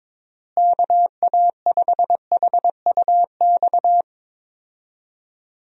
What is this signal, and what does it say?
Morse code, heard as a keyed tone, about 22 wpm, 700 Hz. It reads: KA5HUX
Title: USA Call Signs - 6 Characters - Repeated 2x - 22wpm